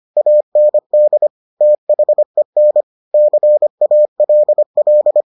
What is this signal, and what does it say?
Morse code, heard as a keyed tone, about 25 wpm, 600 Hz. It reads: AND THEN CALL